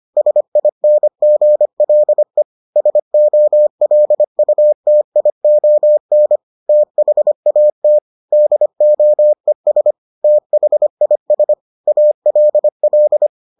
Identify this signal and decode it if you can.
Morse code, heard as a keyed tone, about 25 wpm, 600 Hz. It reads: SINGLE SOLUTION THAT DOES THIS ALL